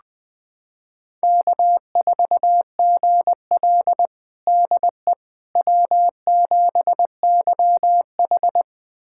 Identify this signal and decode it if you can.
Morse code, heard as a keyed tone, about 20 wpm, 700 Hz. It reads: K4GL DE W7YH
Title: USA Callsign DE Callsign - 4 Characters - 20wpm